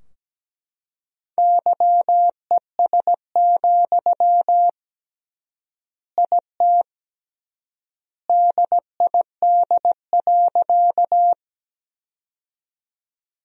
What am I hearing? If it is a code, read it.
Morse code, heard as a keyed tone, about 17 wpm, 700 Hz. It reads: YES, IT DID.
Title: 3x Word-Spacing - Sentences from Top 500 Words - 17wpm